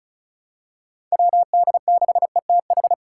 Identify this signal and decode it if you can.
Morse code, heard as a keyed tone, about 34 wpm, 700 Hz. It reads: WD6ETH
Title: USA Call Signs - 6 Characters - 2x Character Spacing - 35wpm